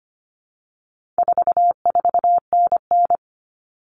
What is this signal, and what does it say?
Morse code, heard as a keyed tone, about 25 wpm, 700 Hz. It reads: 44NN
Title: Common Cut Numbers - 4-Digit 25wpm